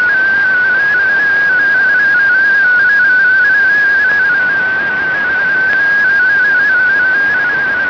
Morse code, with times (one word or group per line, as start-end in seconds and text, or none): none